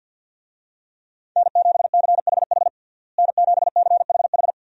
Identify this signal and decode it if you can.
Morse code, heard as a keyed tone, about 47 wpm, 700 Hz. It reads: N7CHH N7CHH
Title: USA Call Signs - 5 Characters - Repeated 2x - 50wpm